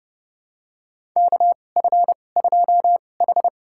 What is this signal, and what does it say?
Morse code, heard as a keyed tone, about 32 wpm, spaced as for 22 wpm, 700 Hz. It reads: KF2H
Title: USA Call Signs - 4 Characters - 2x Character Spacing - 30wpm